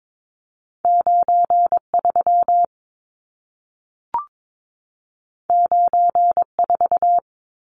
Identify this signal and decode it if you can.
Morse code, heard as a keyed tone, about 22 wpm, 700 Hz. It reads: 93 94
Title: Numbers - 2-Digit 22wpm